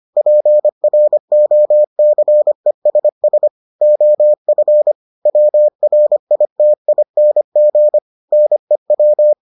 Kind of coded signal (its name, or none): Morse code